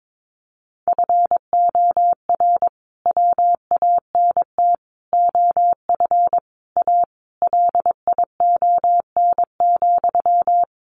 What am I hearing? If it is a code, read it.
Morse code, heard as a keyed tone, about 22 wpm, 700 Hz. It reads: FOR WANT OF A LION,
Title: A Sound Of Thunder 22wpm